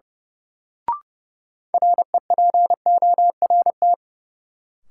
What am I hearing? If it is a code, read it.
Morse code, heard as a keyed tone, about 30 wpm, 700 Hz. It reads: REPORT
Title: Sentences from Top 900 Words - New Word Drill - 30wpm